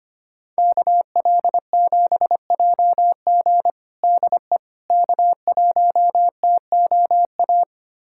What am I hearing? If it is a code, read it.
Morse code, heard as a keyed tone, about 25 wpm, 700 Hz. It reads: KL7JG DE K1TOA
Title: USA Callsign DE Callsign - 5 Characters - 25wpm